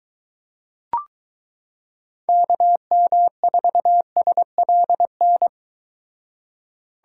Morse code, heard as a keyed tone, about 23 wpm, 700 Hz. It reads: KM4SLN